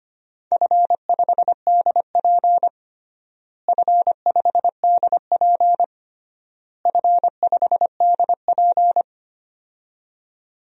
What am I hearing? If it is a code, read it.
Morse code, heard as a keyed tone, about 25 wpm, 700 Hz. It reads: F5DP F5DP F5DP
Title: DXCC-100 Call Signs - 4 Characters - Repeated 3x - 25wpm